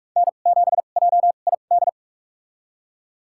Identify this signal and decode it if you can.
Morse code, heard as a keyed tone, about 47 wpm, spaced as for 33 wpm, 700 Hz. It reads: N7JID
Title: USA Call Signs - 5 Characters - 2x Character Spacing - 45wpm